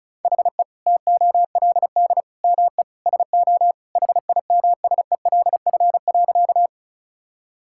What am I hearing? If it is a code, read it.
Morse code, heard as a keyed tone, about 36 wpm, 700 Hz. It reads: HE TOLD ME SO HIMSELF.
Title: Sentences from Top 1600 Words 35wpm